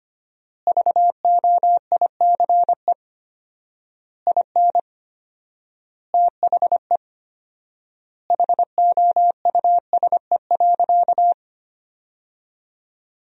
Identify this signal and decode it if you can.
Morse code, heard as a keyed tone, about 25 wpm, 700 Hz. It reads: VOICE IN THE HOUSE.
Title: 4x Word-Spacing - Sentences from Top 500 Words - 25wpm